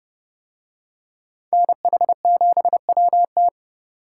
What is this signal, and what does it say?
Morse code, heard as a keyed tone, about 30 wpm, 700 Hz. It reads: NH7WT